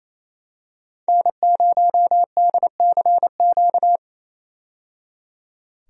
Morse code, heard as a keyed tone, about 28 wpm, 700 Hz. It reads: N0DCQ
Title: USA Call Signs - 5 Characters - 28wpm